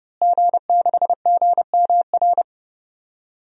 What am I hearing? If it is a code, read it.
Morse code, heard as a keyed tone, about 30 wpm, 700 Hz. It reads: G6GMR
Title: DXCC-100 Call Signs - 5 Characters - Repeated 2x - 30wpm